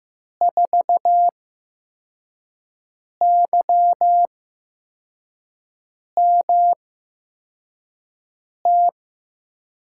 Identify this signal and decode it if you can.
Morse code, heard as a keyed tone, about 15 wpm, spaced as for 4 wpm, 700 Hz. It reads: 4YMT
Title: USA Call Signs - 5 Characters - 8x Character Spacing - 15wpm